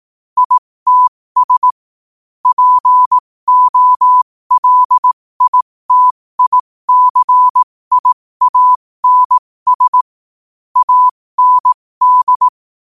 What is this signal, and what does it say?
Morse code, heard as a keyed tone, about 18 wpm, spaced as for 15 wpm, 1000 Hz. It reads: ITS POLITICIANS AND
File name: 15wpm